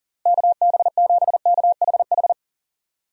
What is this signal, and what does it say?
Morse code, heard as a keyed tone, about 37 wpm, 700 Hz. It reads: KB7KHH
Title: USA Call Signs - 6 Characters - Repeated 2x - 40wpm